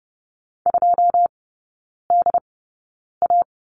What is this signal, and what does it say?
Morse code, heard as a keyed tone, about 30 wpm, spaced as for 9 wpm, 700 Hz. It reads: 2DA
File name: 081 Lesson 14e - Number 2 - Three Characters 30wpm